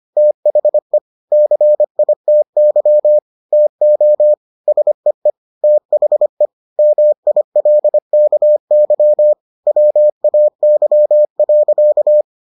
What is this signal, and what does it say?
Morse code, heard as a keyed tone, about 25 wpm, 600 Hz. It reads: THE CITY TO SEE THE MILKY WAY.